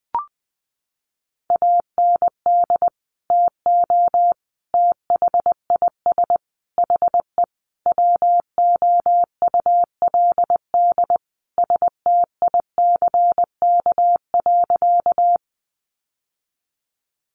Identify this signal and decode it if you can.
Morse code, heard as a keyed tone, about 20 wpm, 700 Hz. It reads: AND TO THIS HE WOULD STICK.
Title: Sentences from Top 1200 Words 20wpm